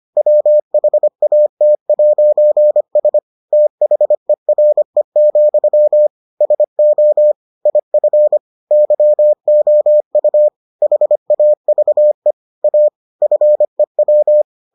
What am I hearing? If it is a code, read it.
Morse code, heard as a keyed tone, about 25 wpm, 600 Hz. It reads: WHAT'S THERE, SO IF YOU HAVE A FEW